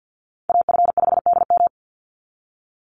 Morse code, heard as a keyed tone, about 50 wpm, 700 Hz. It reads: AF5DM
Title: Call Signs - 50wpm